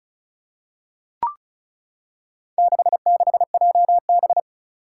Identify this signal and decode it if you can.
Morse code, heard as a keyed tone, about 34 wpm, 700 Hz. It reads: 66JB
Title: Four Challenging Letter-Number Combinations 35wpm